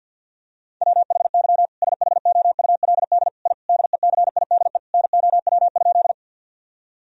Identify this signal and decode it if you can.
Morse code, heard as a keyed tone, about 52 wpm, 700 Hz. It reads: WHY SHOULD I DECIDE NOW?
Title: Sentences from Top 1000 Words - Review Part 3 - 50wpm